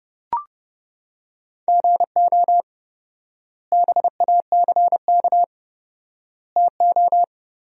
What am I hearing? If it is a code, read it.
Morse code, heard as a keyed tone, about 32 wpm, 700 Hz. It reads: GO BACK TO
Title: 4x Word-Spacing - Sentences from Top 500 Words - 30wpm